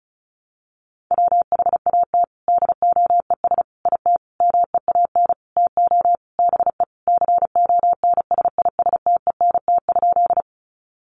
Morse code, heard as a keyed tone, about 34 wpm, 700 Hz. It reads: WHAT DOES IT MEAN TO BE CONSISTENT?
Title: Sentences from Top 1500 Words 35wpm